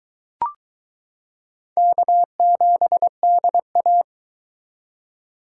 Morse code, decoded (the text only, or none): K7DA